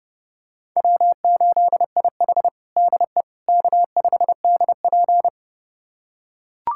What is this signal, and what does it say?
Morse code, heard as a keyed tone, about 32 wpm, 700 Hz. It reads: W8IH DE K5DP
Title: USA Callsign DE Callsign - 4 Characters - 30wpm